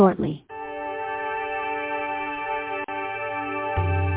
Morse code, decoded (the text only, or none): none